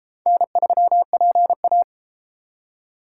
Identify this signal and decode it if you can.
Morse code, heard as a keyed tone, about 32 wpm, 700 Hz. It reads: N3PA